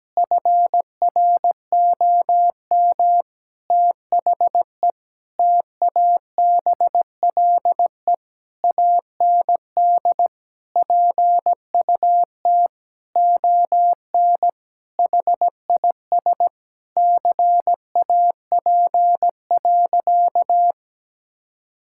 Morse code, heard as a keyed tone, about 17 wpm, 700 Hz. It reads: FROM THE TABLE AND PUT ON HIS CAP.
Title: Sentences from Top 1600 Words 17wpm